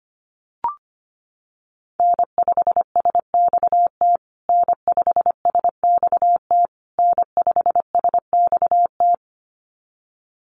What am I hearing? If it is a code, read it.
Morse code, heard as a keyed tone, about 25 wpm, 700 Hz. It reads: N5SXT N5SXT N5SXT
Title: USA Call Signs Repeated 3x 25wpm